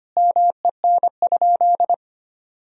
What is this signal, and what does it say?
Morse code, heard as a keyed tone, about 25 wpm, 700 Hz. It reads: MEN?
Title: Sentences from Top 200 Words and Abbreviations 25wpm